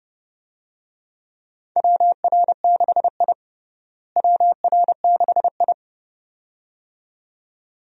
Morse code, heard as a keyed tone, about 30 wpm, 700 Hz. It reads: WR6I WR6I
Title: USA Call Signs - 4 Characters - Repeated 2x - 30wpm